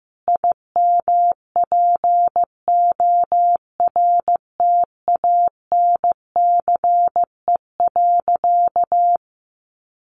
Morse code, heard as a keyed tone, about 15 wpm, 700 Hz. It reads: IMPORTANCE.
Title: Sentences from Top 1200 Words 15wpm